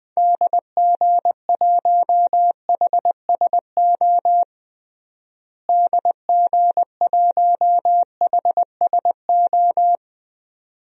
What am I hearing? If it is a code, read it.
Morse code, heard as a keyed tone, about 20 wpm, 700 Hz. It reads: DG1HSO DG1HSO
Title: DXCC-100 Call Signs - 6 Characters - Repeated 2x - 20wpm